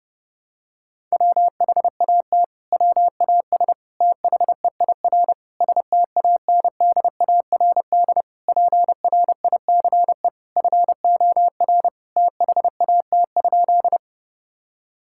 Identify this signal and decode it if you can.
Morse code, heard as a keyed tone, about 30 wpm, 700 Hz. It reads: WHAT WAS THEIR STANDARD PRICE FOR THAT?
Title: Sentences from Top 1000 Words - Review Part 6 - 30wpm